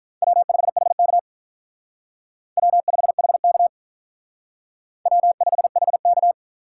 Morse code, heard as a keyed tone, about 45 wpm, 700 Hz. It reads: W5HK W5HK W5HK